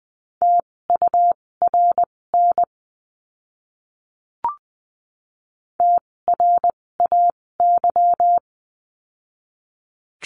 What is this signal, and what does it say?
Morse code, heard as a keyed tone, about 20 wpm, spaced as for 16 wpm, 700 Hz. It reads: TURN TRAY
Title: 256 Lesson 42a - Words 12-20wpm